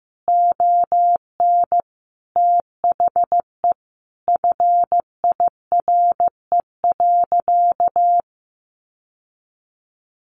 Morse code, heard as a keyed tone, about 15 wpm, 700 Hz. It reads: ON THE FIRE.
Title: Sentences from Top 1400 Words 15wpm